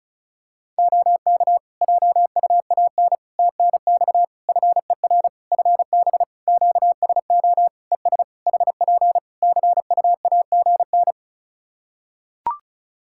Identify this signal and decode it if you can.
Morse code, heard as a keyed tone, about 36 wpm, 700 Hz. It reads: OK JUAN TNX FER FB QSO ES HP CUAGN
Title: Practice QSO - Exchange 4 - Ending - 35wpm